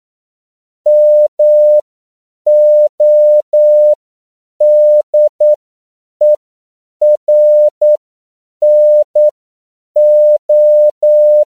Morse code, heard as a keyed tone, about 9 wpm, spaced as for 7 wpm, 600 Hz. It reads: MODERNO